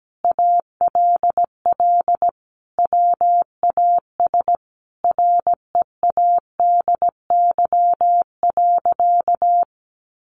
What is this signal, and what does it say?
Morse code, heard as a keyed tone, about 17 wpm, 700 Hz. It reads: ALL WAS READY.
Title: Sentences from Top 1000 Words 17wpm